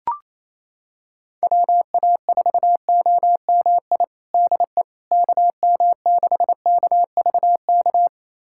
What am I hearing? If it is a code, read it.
Morse code, heard as a keyed tone, about 27 wpm, 700 Hz. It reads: WA4OMI DE KM6KVK
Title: USA Callsign DE Callsign - 6 Characters - 28wpm